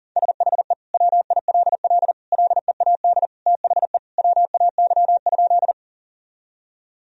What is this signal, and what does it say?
Morse code, heard as a keyed tone, about 40 wpm, 700 Hz. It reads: SHE WILL LEAD THE WAY?
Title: Sentences from Top 1000 Words - Review Part 2 - 40wpm